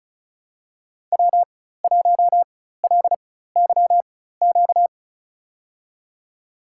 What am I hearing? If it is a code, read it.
Morse code, heard as a keyed tone, about 36 wpm, spaced as for 16 wpm, 700 Hz. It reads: W1LYQ
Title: USA Call Signs - 4x Character Spacing - 35wpm